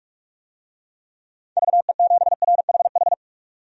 Morse code, heard as a keyed tone, about 45 wpm, 700 Hz. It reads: VE7RHH
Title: DXCC-100 Call Signs - 6 Characters - 45wpm